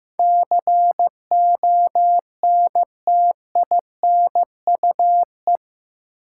Morse code, heard as a keyed tone, about 15 wpm, 700 Hz. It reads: CONTINUE
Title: Sentences from Top 1000 Words - Review Part 6 - New Word Drill - 15wpm